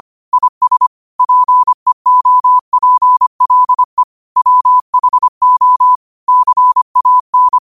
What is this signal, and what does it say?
Morse code, heard as a keyed tone, about 25 wpm, 1000 Hz. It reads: IS PEOPLE WHO CAN